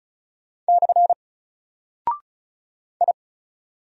Morse code, heard as a keyed tone, about 36 wpm, 700 Hz. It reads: / I
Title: Single Character (Letter:Number:Punctuation:Prosign) 35wpm